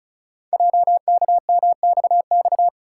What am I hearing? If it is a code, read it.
Morse code, heard as a keyed tone, about 36 wpm, 690 Hz. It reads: JKMXX
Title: Code Groups - 5 Letters - 35wpm